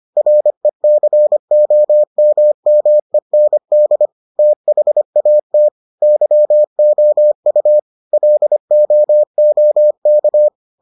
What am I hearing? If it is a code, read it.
Morse code, heard as a keyed tone, about 25 wpm, 600 Hz. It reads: RECOMMEND THAT YOU LOOK